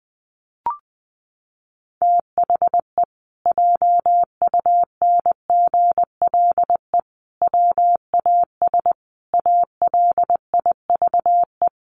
Morse code, heard as a keyed tone, about 20 wpm, 700 Hz. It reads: THE JUNGLE WAS ALIVE
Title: A Sound Of Thunder 20wpm